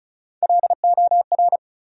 Morse code, heard as a keyed tone, about 36 wpm, 700 Hz. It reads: LOR
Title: USA Call Sign Postfixes - 3 Letters 35wpm